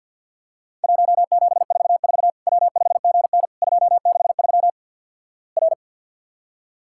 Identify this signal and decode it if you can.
Morse code, heard as a keyed tone, about 50 wpm, 700 Hz. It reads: 1744 W5GN 263 R
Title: Worked All Europe DX Contest - QTC Exchange - 50wpm